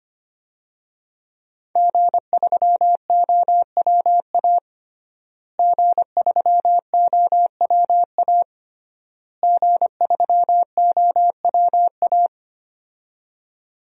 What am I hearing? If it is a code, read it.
Morse code, heard as a keyed tone, about 25 wpm, 700 Hz. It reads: G3OWA G3OWA G3OWA
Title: DXCC-100 Call Signs - 5 Characters - Repeated 3x - 25wpm